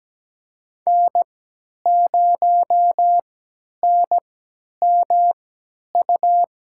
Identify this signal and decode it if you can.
Morse code, heard as a keyed tone, about 17 wpm, spaced as for 10 wpm, 700 Hz. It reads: N0NMU
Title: USA Call Signs - 3x Character Spacing - 17wpm